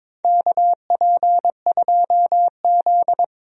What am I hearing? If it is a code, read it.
Morse code, heard as a keyed tone, about 22 wpm, 700 Hz. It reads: KP2Z
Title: DXCC-100 Call Signs - 4 Characters - 22wpm